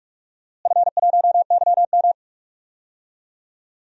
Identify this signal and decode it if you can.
Morse code, heard as a keyed tone, about 46 wpm, 700 Hz. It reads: F1YM
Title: DXCC-100 Call Signs - 4 Characters - Repeated 2x - 45wpm